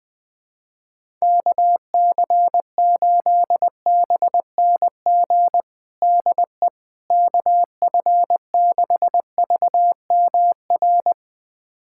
Morse code, heard as a keyed tone, about 20 wpm, 700 Hz. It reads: KC8BNG DE KF6VMR
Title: USA Callsign DE Callsign - 6 Characters - 20wpm